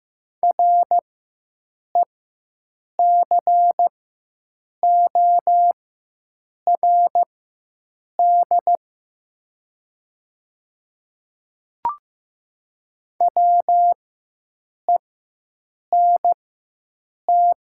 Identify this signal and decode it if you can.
Morse code, heard as a keyed tone, about 15 wpm, spaced as for 7 wpm, 700 Hz. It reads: RECORD WENT
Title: 4x Character Spacing - Top 500 Words - 15wpm